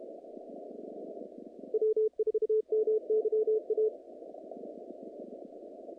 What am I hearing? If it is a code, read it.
Morse code, heard as a keyed tone, about 32 wpm, 430 Hz. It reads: W4MYA
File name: contests / naqp-cw-k5tr-op-k5pi- -2300